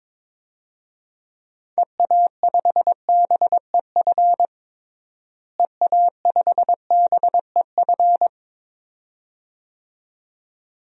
Morse code, heard as a keyed tone, about 21 wpm, 700 Hz. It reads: EA5BEF EA5BEF
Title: DXCC-100 Call Signs - 6 Characters - Repeated 2x - 22wpm